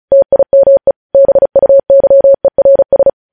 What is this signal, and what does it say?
Morse code, heard as a keyed tone, about 36 wpm, 550 Hz. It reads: TIME BUYERS
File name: podcast / News headlines-1x-35x35